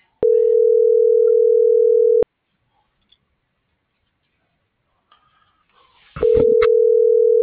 Ringback tone (North America), 440 + 480 Hz.